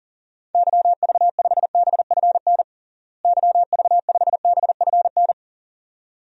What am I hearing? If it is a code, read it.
Morse code, heard as a keyed tone, about 40 wpm, 700 Hz. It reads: YV5BFN YV5BFN
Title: DXCC-100 Call Signs - 6 Characters - Repeated 2x - 40wpm